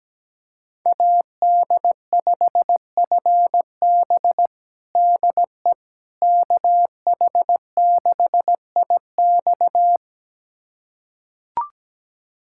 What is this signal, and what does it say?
Morse code, heard as a keyed tone, about 17 wpm, 700 Hz. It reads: AD5FB DE KH6IX
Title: USA Callsign DE Callsign - 5 Characters - 17wpm